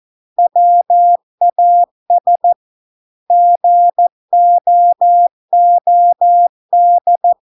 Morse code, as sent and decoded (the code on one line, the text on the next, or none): .-- .- ... / --. --- --- -..
WAS GOOD